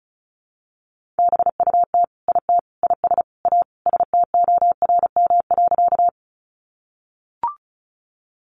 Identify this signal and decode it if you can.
Morse code, heard as a keyed tone, about 34 wpm, 700 Hz. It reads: BUT IT IS A STORM.
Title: Sentences from Top 1500 Words 35wpm